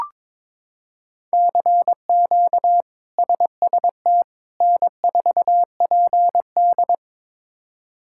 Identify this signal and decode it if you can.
Morse code, heard as a keyed tone, about 22 wpm, 700 Hz. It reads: CQ SST N4PD
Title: K1USN SST QSO Practice - 22wpm